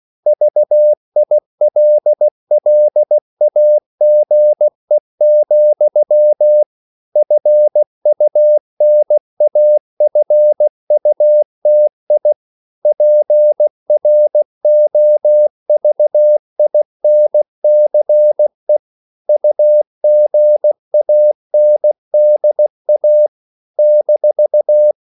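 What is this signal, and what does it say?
Morse code, heard as a keyed tone, about 16 wpm, 600 Hz. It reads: VILLAGE, FUNAFUTI PROVINCE UGANDA -
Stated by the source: CW practice of national capitals at 16 words per minute, from ON6ZQ.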